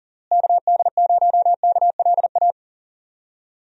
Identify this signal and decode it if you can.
Morse code, heard as a keyed tone, about 40 wpm, 700 Hz. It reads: KD0KLA
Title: USA Call Signs - 6 Characters - Repeated 3x - 40wpm